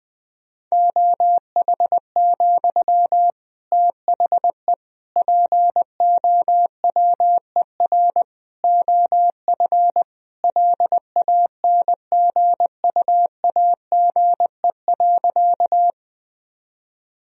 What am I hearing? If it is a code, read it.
Morse code, heard as a keyed tone, about 20 wpm, 700 Hz. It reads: OH, THE POWER OF LANGUAGE.
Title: Sentences from Top 1000 Words - Review Part 3 - 20wpm